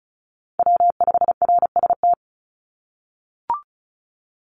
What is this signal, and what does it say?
Morse code, heard as a keyed tone, about 36 wpm, 700 Hz. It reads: W5RST
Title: NA SOTA Chaser Callsigns 35wpm